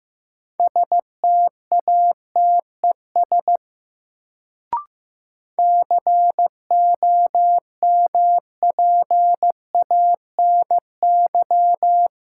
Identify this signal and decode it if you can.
Morse code, heard as a keyed tone, about 15 wpm, 700 Hz. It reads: STATES COMPANY
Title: Sets of 2 Words - New Word Drill - 15wpm